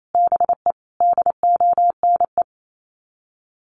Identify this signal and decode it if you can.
Morse code, heard as a keyed tone, about 28 wpm, 700 Hz. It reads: BE DONE
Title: Sets of 5 Words Encore 28wpm